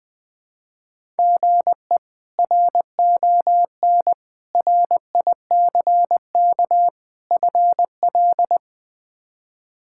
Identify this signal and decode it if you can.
Morse code, heard as a keyed tone, about 20 wpm, 700 Hz. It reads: GE RON RICK FL
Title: K1USN SST QSO Practice - 20wpm